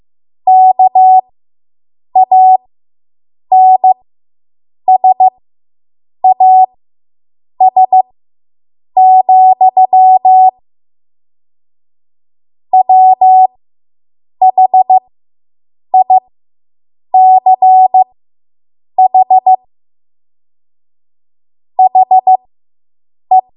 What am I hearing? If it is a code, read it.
Morse code, heard as a keyed tone, about 15 wpm, spaced as for 7 wpm, 750 Hz. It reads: KANSAS, WHICH HE